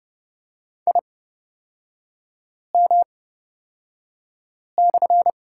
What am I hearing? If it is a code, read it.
Morse code, heard as a keyed tone, about 30 wpm, spaced as for 5 wpm, 690 Hz. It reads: IM/
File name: Lesson 37e - Letter x - Three Characters 30wpm